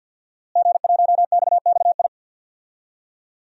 Morse code, heard as a keyed tone, about 53 wpm, 700 Hz. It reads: G1XXI
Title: DXCC-100 Call Signs - 5 Characters - Repeated 2x - 50wpm